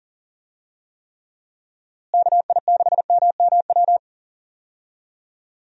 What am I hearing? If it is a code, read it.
Morse code, heard as a keyed tone, about 37 wpm, 700 Hz. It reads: KI6MMW
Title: USA Call Signs - 6 Characters - Repeated 2x - 40wpm